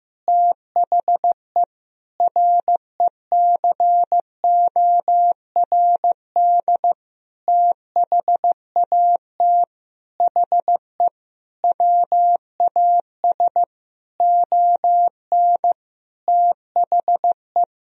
Morse code, heard as a keyed tone, about 15 wpm, 700 Hz. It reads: THE RECORD THAT HE WAS ON THE
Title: Sentences from Top 500 Words and Abbreviations 15wpm